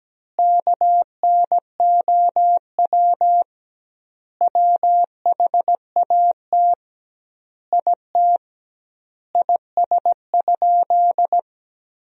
Morse code, heard as a keyed tone, about 17 wpm, 700 Hz. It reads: KNOW WHAT IT IS?
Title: 2x Word-Spacing - Sentences from Top 100 Words - 17wpm